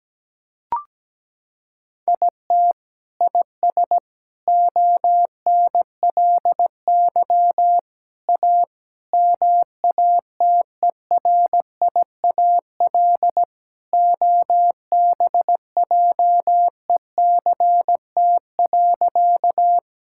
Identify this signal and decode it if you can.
Morse code, heard as a keyed tone, about 17 wpm, 700 Hz. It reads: IT IS ONLY A MATERIAL OBJECT.
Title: Sentences from Top 1000 Words - Review Part 5 - 17wpm